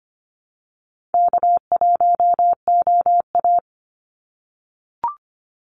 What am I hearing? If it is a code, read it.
Morse code, heard as a keyed tone, about 25 wpm, 700 Hz. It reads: K1OA
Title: Easy Call Signs - 25wpm